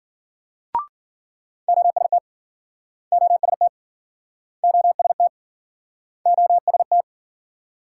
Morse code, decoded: OST OST OST OST

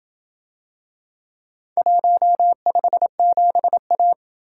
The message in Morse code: 157A

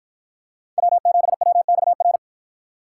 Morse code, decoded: W7WXR

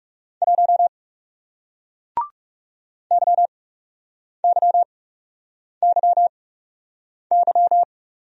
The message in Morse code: 1 YYYY

